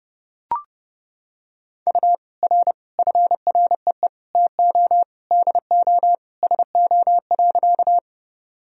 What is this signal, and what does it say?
Morse code, heard as a keyed tone, about 30 wpm, 700 Hz. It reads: U R FREE TO DO SO.
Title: Sentences from Top 500 Words and Abbreviations 30wpm